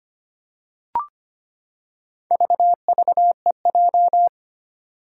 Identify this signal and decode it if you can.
Morse code, heard as a keyed tone, about 25 wpm, 700 Hz. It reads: VVEJ
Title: Four Challenging Letter-Number Combinations 25wpm